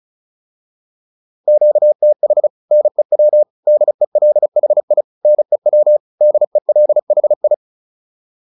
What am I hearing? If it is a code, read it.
Morse code, heard as a keyed tone, about 34 wpm, 600 Hz. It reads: QTH NEW DELHI NEW DELHI
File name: QSO410 35